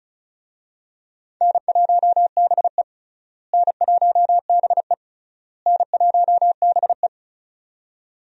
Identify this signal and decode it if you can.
Morse code, heard as a keyed tone, about 34 wpm, 700 Hz. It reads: N1BE N1BE N1BE
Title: USA Call Signs - 4 Characters - Repeated 3x - 35wpm